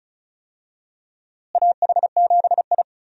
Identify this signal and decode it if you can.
Morse code, heard as a keyed tone, about 34 wpm, 700 Hz. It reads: AH7I
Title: USA Call Signs - 4 Characters - Repeated 3x - 35wpm